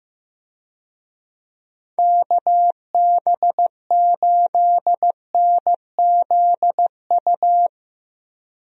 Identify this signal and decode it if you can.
Morse code, heard as a keyed tone, about 15 wpm, 700 Hz. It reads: KB8NZU
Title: USA Call Signs - 6 Characters - 15wpm